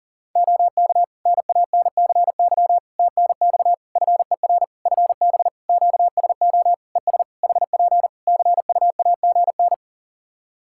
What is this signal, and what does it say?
Morse code, heard as a keyed tone, about 40 wpm, 700 Hz. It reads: OK NANCY TNX FER FB QSO ES HP CUAGN
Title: Practice QSO - Exchange 4 - Ending - 40wpm